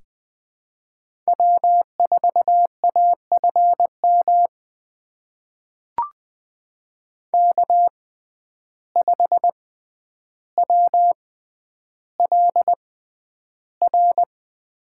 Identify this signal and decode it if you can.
Morse code, heard as a keyed tone, about 20 wpm, spaced as for 10 wpm, 700 Hz. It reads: W4AFM K5WLR
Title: USA Call Signs - 5 Characters - 6x Character Spacing - 20wpm